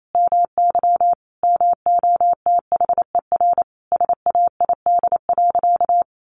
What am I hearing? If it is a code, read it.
Morse code, heard as a keyed tone, about 28 wpm, 700 Hz. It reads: MY MOTHER SAID.
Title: Sentences from Top 700 Words 28wpm